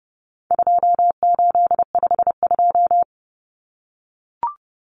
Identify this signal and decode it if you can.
Morse code, heard as a keyed tone, about 30 wpm, 700 Hz. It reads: 2852
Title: Common Cut Numbers - 4-Digit 30wpm